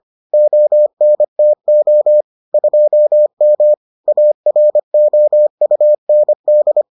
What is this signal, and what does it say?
Morse code, heard as a keyed tone, about 25 wpm, 600 Hz. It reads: ONTO 2M AROUND